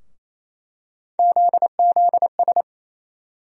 Morse code, heard as a keyed tone, about 28 wpm, 700 Hz. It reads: ZZS
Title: USA Call Sign Postfixes - 3 Letters 28wpm